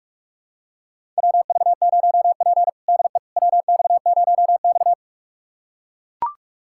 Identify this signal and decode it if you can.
Morse code, heard as a keyed tone, about 41 wpm, 700 Hz. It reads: WV0P DE WX0X
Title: USA Callsign DE Callsign - 4 Characters - 45wpm